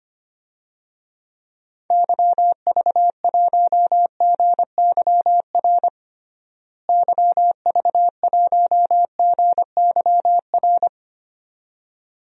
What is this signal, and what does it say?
Morse code, heard as a keyed tone, about 25 wpm, 700 Hz. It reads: YV1GYR YV1GYR
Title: DXCC-100 Call Signs - 6 Characters - Repeated 2x - 25wpm